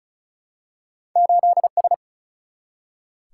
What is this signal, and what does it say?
Morse code, heard as a keyed tone, about 36 wpm, 700 Hz. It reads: 8 S
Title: Single-Letter Single-Number 35wpm